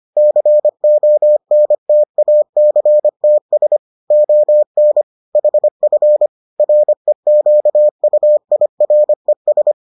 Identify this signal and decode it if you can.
Morse code, heard as a keyed tone, about 25 wpm, 600 Hz. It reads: CONTACTS ON HF REQUIRES